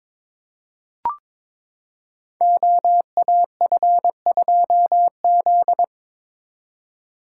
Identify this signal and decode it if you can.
Morse code, heard as a keyed tone, about 22 wpm, 700 Hz. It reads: OAF2Z
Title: Code Groups - 5 Letter:Numbers - 22wpm